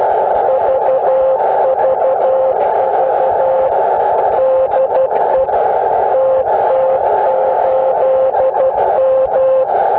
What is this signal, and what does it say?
Morse code, heard as a keyed tone, about 12 wpm, 530 Hz, with noise. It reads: VVV DE TT8M